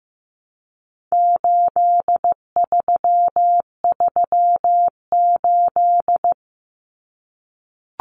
Morse code, heard as a keyed tone, about 15 wpm, 700 Hz. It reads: 8338
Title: Numbers - 4-Digit 15wpm